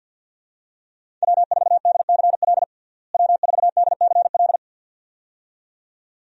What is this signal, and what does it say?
Morse code, heard as a keyed tone, about 52 wpm, 700 Hz. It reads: W4DCL W4DCL
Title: USA Call Signs - 5 Characters - Repeated 2x - 50wpm